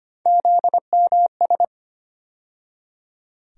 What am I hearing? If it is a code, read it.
Morse code, heard as a keyed tone, about 25 wpm, 700 Hz. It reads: ZMS
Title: USA Call Sign Postfixes - 3 Letters 25wpm